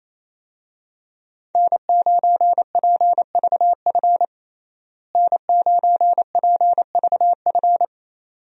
Morse code, heard as a keyed tone, about 28 wpm, 700 Hz. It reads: N9PVF N9PVF
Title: USA Call Signs - 5 Characters - Repeated 2x - 28wpm